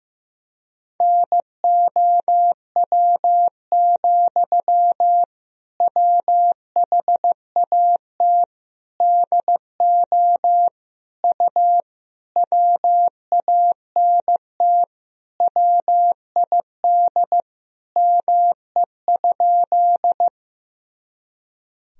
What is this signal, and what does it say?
Morse code, heard as a keyed tone, about 15 wpm, 700 Hz. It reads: NOW, WHAT DO U WANT WID ME?
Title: Sentences from Top 200 Words and Abbreviations 15wpm